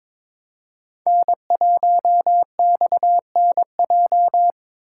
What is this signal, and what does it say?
Morse code, heard as a keyed tone, about 22 wpm, 690 Hz. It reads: N1XNJ
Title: USA Call Signs - 5 Characters - 2x Character Spacing - 22wpm